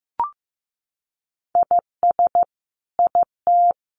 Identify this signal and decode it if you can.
Morse code, heard as a keyed tone, about 15 wpm, 700 Hz. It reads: IS IT
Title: Sentences from Top 1200 Words 15wpm